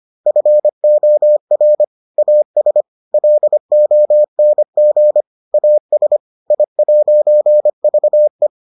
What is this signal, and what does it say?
Morse code, heard as a keyed tone, about 25 wpm, 600 Hz. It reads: FOR AS LONG AS I'VE